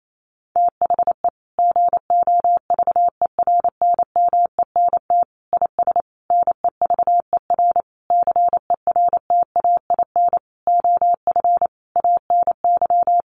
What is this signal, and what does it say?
Morse code, heard as a keyed tone, about 28 wpm, 700 Hz. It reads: THE GOVERNMENT IS NEVER CERTAIN OF ANY
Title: Sentences from Top 1300 Words 28wpm